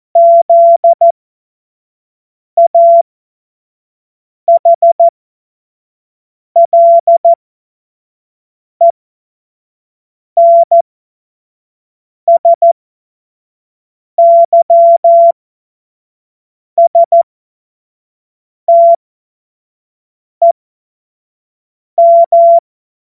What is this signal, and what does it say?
Morse code, heard as a keyed tone, about 14 wpm, spaced as for 5 wpm, 680 Hz. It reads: ZAHLENSYSTEM
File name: Morsekurs / Witze / Witz 14. Dezember 2021, 25 F70